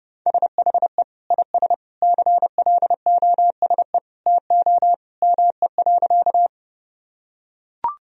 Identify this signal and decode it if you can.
Morse code, heard as a keyed tone, about 32 wpm, 700 Hz. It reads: SHE IS CLOSE TO ME.
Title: Sentences from Top 1000 Words - Review Part 1 - 30wpm